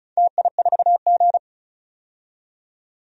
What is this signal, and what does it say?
Morse code, heard as a keyed tone, about 36 wpm, 700 Hz. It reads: TI4G
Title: DXCC-100 Call Signs - 4 Characters - 35wpm